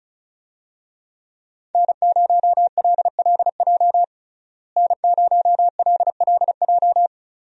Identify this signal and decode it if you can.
Morse code, heard as a keyed tone, about 36 wpm, 700 Hz. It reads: N0LLJ N0LLJ
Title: USA Call Signs - 5 Characters - Repeated 2x - 35wpm